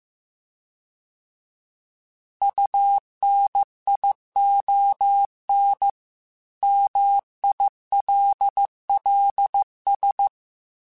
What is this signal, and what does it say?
Morse code, heard as a keyed tone, about 15 wpm, 800 Hz. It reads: UNION MILLS